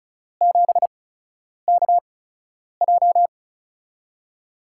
Morse code, heard as a keyed tone, about 36 wpm, 700 Hz. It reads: 7 K J
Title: USA Call Signs - 4 Characters - 8x Character Spacing - 35wpm